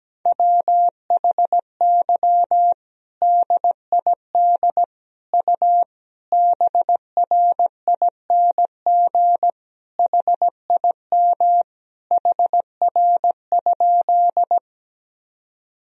Morse code, heard as a keyed tone, about 17 wpm, 700 Hz. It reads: WHY DID U BRING HIM HR?
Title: Sentences from Top 500 Words and Abbreviations 17wpm